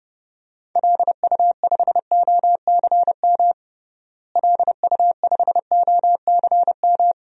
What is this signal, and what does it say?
Morse code, heard as a keyed tone, about 30 wpm, 700 Hz. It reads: LU5OCM LU5OCM
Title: DXCC-100 Call Signs - 6 Characters - Repeated 2x - 30wpm